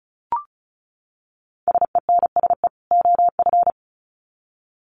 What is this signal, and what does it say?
Morse code, heard as a keyed tone, about 34 wpm, 700 Hz. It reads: SENSE OF
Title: Sets of 2 Words 35wpm